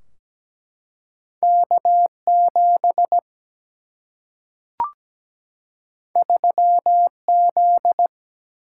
Morse code, heard as a keyed tone, about 17 wpm, 700 Hz. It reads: K7 3Z
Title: Single-Letter Single-Number 17wpm